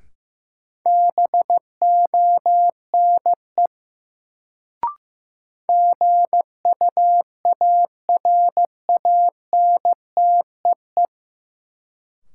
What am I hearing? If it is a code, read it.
Morse code, heard as a keyed tone, about 15 wpm, 700 Hz. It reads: BONE GUARANTEE
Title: Sentences from Top 1600 Words - New Word Drill Alt - 15wpm